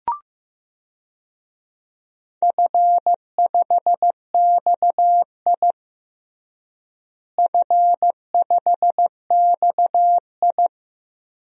Morse code, heard as a keyed tone, about 16 wpm, 700 Hz. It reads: F5XI F5XI
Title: DXCC-100 Call Signs - 4 Characters - Repeated 2x - 15wpm